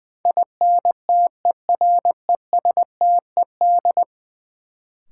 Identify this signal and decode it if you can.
Morse code, heard as a keyed tone, about 20 wpm, 700 Hz. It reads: INTERESTED
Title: Sentences from Top 900 Words - New Word Drill - 20wpm